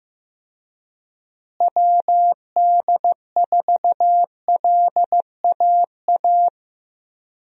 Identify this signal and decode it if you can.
Morse code, heard as a keyed tone, about 15 wpm, 700 Hz. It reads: WD4LAA
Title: USA Call Signs - 6 Characters - Repeated 3x - 15wpm